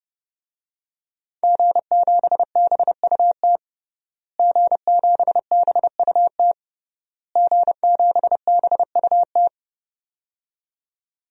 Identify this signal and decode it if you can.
Morse code, heard as a keyed tone, about 32 wpm, 700 Hz. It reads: G7BUT G7BUT G7BUT
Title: DXCC-100 Call Signs - 5 Characters - Repeated 3x - 30wpm